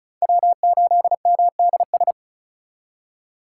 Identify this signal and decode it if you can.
Morse code, heard as a keyed tone, about 34 wpm, 700 Hz. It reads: W8MDS
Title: USA Call Signs - 5 Characters - 4x Character Spacing - 35wpm